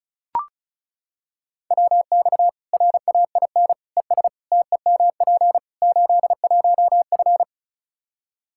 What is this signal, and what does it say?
Morse code, heard as a keyed tone, about 34 wpm, 700 Hz. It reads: WX RAIN ES TEMP 81F
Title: Practice QSO - Exchange 2 - Info - 35wpm